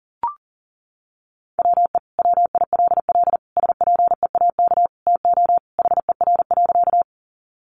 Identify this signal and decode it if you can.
Morse code, heard as a keyed tone, about 40 wpm, 700 Hz. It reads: WE WILL SPEAK TO HER.
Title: Sentences from Top 1100 Words 40wpm